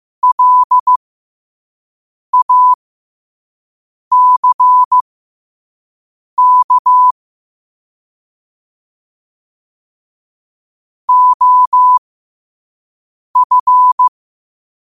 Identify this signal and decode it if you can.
Morse code, heard as a keyed tone, about 15 wpm, spaced as for 5 wpm, 1000 Hz. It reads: LACK OF